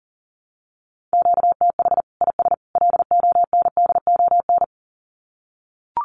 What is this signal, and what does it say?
Morse code, heard as a keyed tone, about 40 wpm, 700 Hz. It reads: QTH IS LONDON
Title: Common QSO Phrases 40wpm